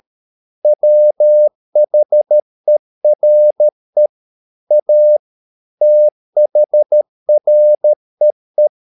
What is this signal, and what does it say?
Morse code, heard as a keyed tone, about 13 wpm, 600 Hz. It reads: WHERE A THREE
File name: mp3 files / Lexiphiles - 13wpm